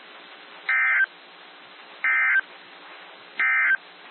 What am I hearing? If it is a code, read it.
Emergency Alert System (SAME) tones, heard as harsh buzzing bursts. It is heard from 0.6 to 3.9 s.
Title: NOAA Weather Radio - WXJ73